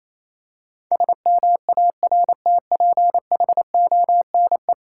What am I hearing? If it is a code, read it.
Morse code, heard as a keyed tone, about 27 wpm, 690 Hz. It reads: SMARTPHONE
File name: Easy Words - Speed-Racing with Hints - 22wpm